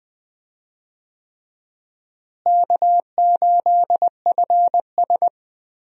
Morse code, heard as a keyed tone, about 20 wpm, 700 Hz. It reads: K8FS
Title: USA Call Signs - 4 Characters - Repeated 3x - 20wpm